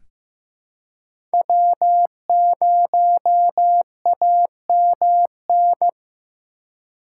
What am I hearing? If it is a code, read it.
Morse code, heard as a keyed tone, about 16 wpm, 700 Hz. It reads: W0AMN
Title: USA Call Signs - 4x Character Spacing - 15wpm